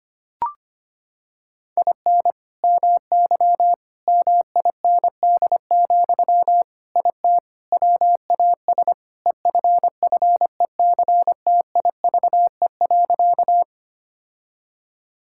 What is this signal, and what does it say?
Morse code, heard as a keyed tone, about 25 wpm, 700 Hz. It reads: IN MY MIND, IT WAS EFFECTIVE.
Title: Sentences from Top 1000 Words - Review Part 6 - 25wpm